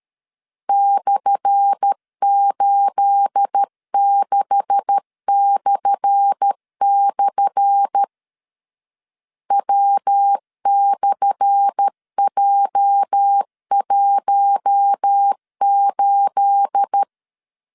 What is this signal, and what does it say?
Morse code, heard as a keyed tone, about 13 wpm, 800 Hz. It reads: /86// W/J18
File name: LEZIONI / 10 SIGNS / 10A SLASH 12wpm 2314